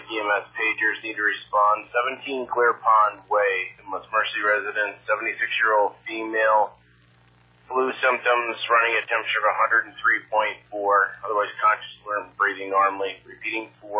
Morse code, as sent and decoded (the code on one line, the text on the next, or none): none